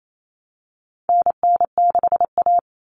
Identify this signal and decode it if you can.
Morse code, heard as a keyed tone, about 28 wpm, 700 Hz. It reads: NN6A
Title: Common Cut Numbers - 4-Digit 28wpm